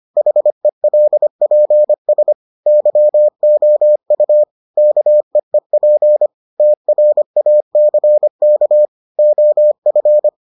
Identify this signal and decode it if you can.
Morse code, heard as a keyed tone, about 25 wpm, 600 Hz. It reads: HELPS YOU KEEP TRACK OF